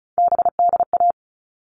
Morse code, heard as a keyed tone, about 34 wpm, 700 Hz. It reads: BDA
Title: All Cut Numbers - 3-Digit 35wpm